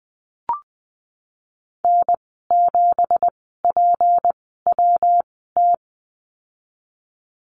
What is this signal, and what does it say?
Morse code, heard as a keyed tone, about 20 wpm, spaced as for 15 wpm, 700 Hz. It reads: N7PWT
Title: 193 Lesson 31g - Letter v - Call Signs 20wpm